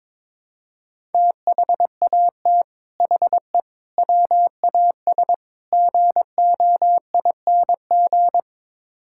Morse code, heard as a keyed tone, about 22 wpm, 700 Hz. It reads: THAT HE WAS GOING
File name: practice / 22wpm / Build A Phrase - 5-Word Phrases - 22wpm